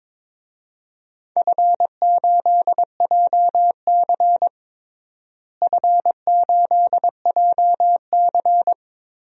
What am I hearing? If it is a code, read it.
Morse code, heard as a keyed tone, about 22 wpm, 700 Hz. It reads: F8JC F8JC
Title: DXCC-100 Call Signs - 4 Characters - Repeated 2x - 22wpm